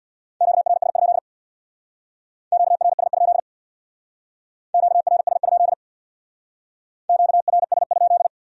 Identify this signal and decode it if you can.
Morse code, heard as a keyed tone, about 60 wpm, 700 Hz. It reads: QRS? QRS? QRS? QRS?
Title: Speed-Racing Q-Codes - 50wpm